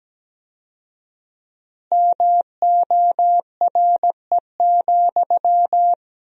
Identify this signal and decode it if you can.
Morse code, heard as a keyed tone, about 17 wpm, 700 Hz. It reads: MORE,
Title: 4x Word-Spacing - Sentences from Top 300 Words - 17wpm